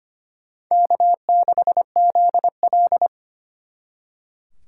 Morse code, heard as a keyed tone, about 32 wpm, 700 Hz. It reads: K6ZL